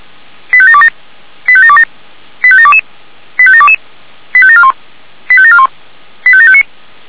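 Five-tone selective call (ZVEI), heard as quick runs of five tones, tones of about 70 ms. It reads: 85727 85727 85722 85722 85741 85741 85759
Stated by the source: ZVEI five-tone call received by SDR.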